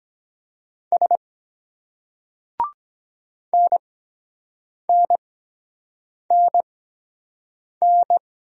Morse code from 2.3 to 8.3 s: NNNN

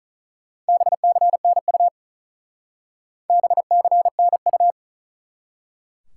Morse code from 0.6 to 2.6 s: BCNU

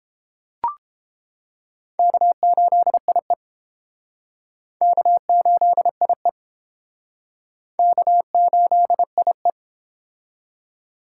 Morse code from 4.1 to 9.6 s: K8IE K8IE

